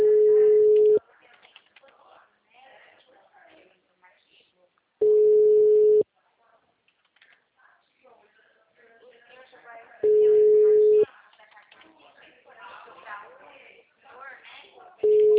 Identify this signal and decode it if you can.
Ringback tone (Europe, 425 Hz), with speech.
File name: out- -2022- - - .69038